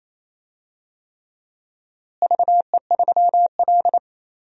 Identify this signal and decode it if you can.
Morse code, heard as a keyed tone, about 28 wpm, 700 Hz. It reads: VE3L
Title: DXCC-100 Call Signs - 4 Characters - Repeated 3x - 28wpm